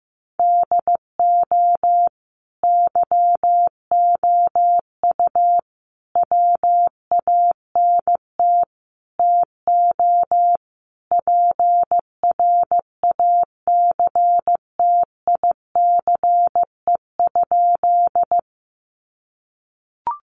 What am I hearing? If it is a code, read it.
Morse code, heard as a keyed tone, about 15 wpm, 700 Hz. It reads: DO YOU WANT TO PRACTICE?
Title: Sentences from Top 700 Words 15wpm